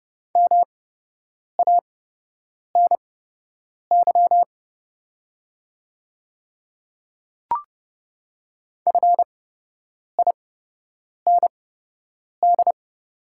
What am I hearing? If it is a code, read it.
Morse code, heard as a keyed tone, about 30 wpm, spaced as for 8 wpm, 700 Hz. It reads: MANY FIND
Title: 8x Character Spacing - Top 100 Words - 30wpm